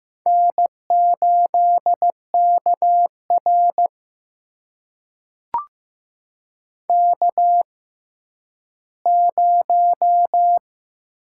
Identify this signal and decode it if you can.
Morse code, heard as a keyed tone, about 15 wpm, spaced as for 10 wpm, 700 Hz. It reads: N8KR K0
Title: USA Call Signs - 4 Characters - 6x Character Spacing - 15wpm